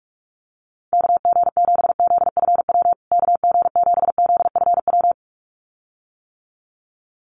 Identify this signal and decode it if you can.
Morse code, heard as a keyed tone, about 44 wpm, 700 Hz. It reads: KG7ZFW KG7ZFW
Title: USA Call Signs Repeated 2x 45wpm